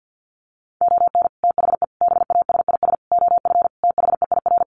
Morse code, heard as a keyed tone, about 50 wpm, 700 Hz. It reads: ON THE BASIS OF THEIR
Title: Sets of 5 Words Encore 50wpm